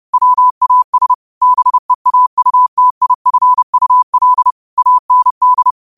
Morse code, heard as a keyed tone, about 30 wpm, 1000 Hz. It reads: WAS BEAUTIFUL AND
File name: qotdfiles / 30wpm